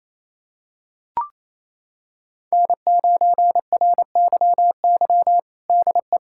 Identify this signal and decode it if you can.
Morse code, heard as a keyed tone, about 27 wpm, 700 Hz. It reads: N9RYY DE
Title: USA Callsign DE Callsign 28wpm